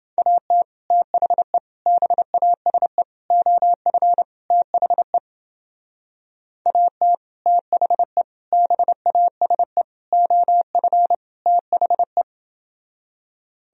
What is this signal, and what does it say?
Morse code, heard as a keyed tone, about 30 wpm, 700 Hz. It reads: AT THE BASE OF THE AT THE BASE OF THE